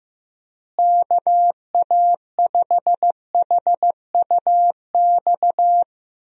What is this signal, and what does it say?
Morse code, heard as a keyed tone, about 15 wpm, 700 Hz. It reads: KA5HUX
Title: USA Call Signs - 6 Characters - Repeated 2x - 15wpm